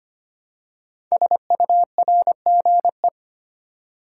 Morse code, heard as a keyed tone, about 25 wpm, 700 Hz. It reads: SURGE
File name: 5 Letter Words - First 3 Letters Given - 25wpm